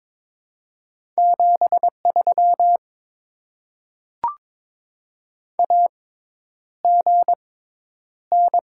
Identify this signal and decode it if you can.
Morse code, heard as a keyed tone, about 22 wpm, spaced as for 10 wpm, 700 Hz. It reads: 73 AGN
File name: Top 20 QSO Elements - 6x Character Spacing - 22wpm